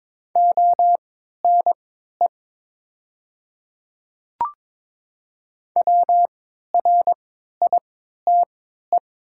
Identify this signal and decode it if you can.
Morse code, heard as a keyed tone, about 23 wpm, spaced as for 13 wpm, 700 Hz. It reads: ONE WRITE
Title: 3x Character Spacing - Top 500 Words - 22wpm